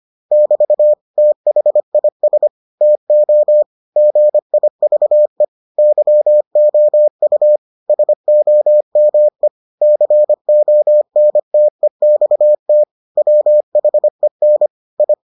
Morse code, heard as a keyed tone, about 25 wpm, 600 Hz. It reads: <BT> THIS TO GIVE YOU SOME CONTEXT WHEN I